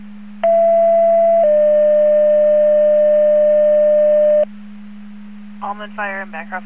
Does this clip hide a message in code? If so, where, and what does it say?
Yes: two-tone page from 0.4 to 4.4 s, heard as a one-second tone, then a three-second tone, about 670 then 600 Hz, with speech.